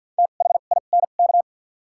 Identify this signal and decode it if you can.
Morse code, heard as a keyed tone, about 51 wpm, spaced as for 36 wpm, 700 Hz. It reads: THINK
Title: 2x Character Spacing - Top 200 Words - 50wpm